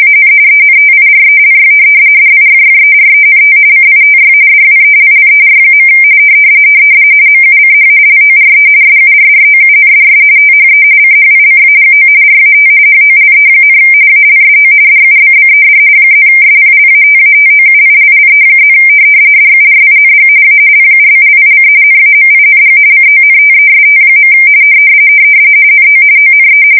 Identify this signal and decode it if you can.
RTTY, about 45.45 baud. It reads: AN ENTIRELY DIFFERENT CITY DWELLER. MUCH LORE SURROUNDS THE CHICAGO RAT HOLE, ALSO DUBBED +SPLATATOUILLE+ DURING A PUBLIC NAMING CONTEST. NO ONE
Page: 170 Shift 2125/2295